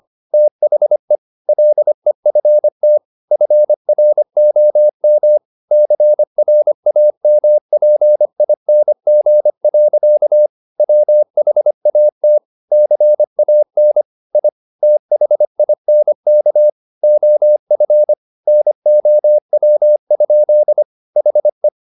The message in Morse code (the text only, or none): THE LEFT FROM CRAMPING. WHAT CAN I THINK OF NOW? HE